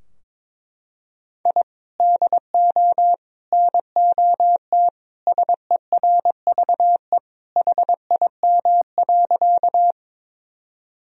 Morse code, heard as a keyed tone, about 22 wpm, 700 Hz. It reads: I DO NOT SERVE HIM.
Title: Sentences from Top 1000 Words - Review Part 2 - 22wpm